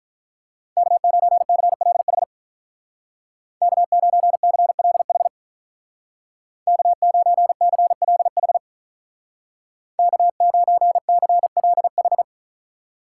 Morse code, heard as a keyed tone, about 43 wpm, 700 Hz. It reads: K9CLH K9CLH K9CLH K9CLH
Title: Speed Racing - USA Call Signs 35wpm